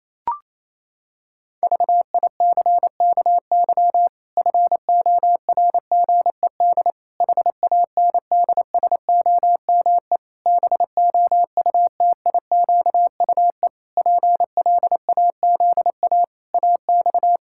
Morse code, heard as a keyed tone, about 27 wpm, 700 Hz. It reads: VICKY FORGED HANDSOME BOUTIQUE PLAZA AX
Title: Pangram Challenge v2 - 28wpm